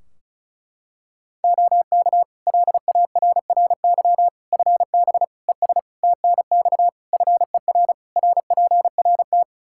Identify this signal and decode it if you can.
Morse code, heard as a keyed tone, about 36 wpm, 700 Hz. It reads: OK LARRY FB ES TNX FER RPRT
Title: Practice QSO - Exchange 2 - Info - 35wpm